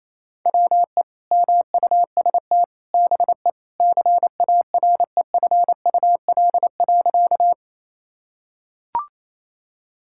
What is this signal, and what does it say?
Morse code, heard as a keyed tone, about 28 wpm, 700 Hz. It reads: WE MUST BE CAREFUL.
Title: Sentences from Top 1600 Words 28wpm